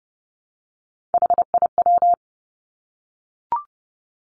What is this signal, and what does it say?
Morse code, heard as a keyed tone, about 30 wpm, 700 Hz. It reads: HIW
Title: Three Challenging Letter-Number Combinations 30wpm